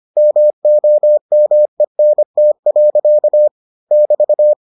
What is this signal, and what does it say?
Morse code, heard as a keyed tone, about 25 wpm, 600 Hz. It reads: MOMENT. <BT>